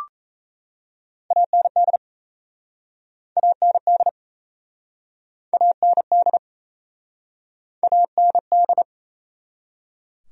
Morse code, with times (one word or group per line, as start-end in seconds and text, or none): 1.3-2.0 AND
3.4-4.1 AND
5.5-6.4 AND
7.8-8.8 AND